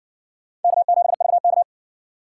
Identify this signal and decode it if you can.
Morse code, heard as a keyed tone, about 60 wpm, 700 Hz. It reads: K7VK